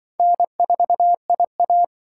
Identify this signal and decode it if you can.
Morse code, heard as a keyed tone, about 24 wpm, 700 Hz. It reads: N4IA
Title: Speed Racing - USA Call Signs 20wpm